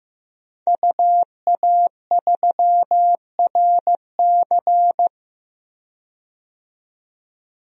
Morse code, heard as a keyed tone, about 15 wpm, 700 Hz. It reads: UA3RC
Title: DXCC-100 Call Signs - 5 Characters - 15wpm